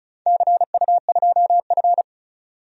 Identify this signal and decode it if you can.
Morse code, heard as a keyed tone, about 36 wpm, 700 Hz. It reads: CU2F
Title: DXCC-100 Call Signs - 4 Characters - 35wpm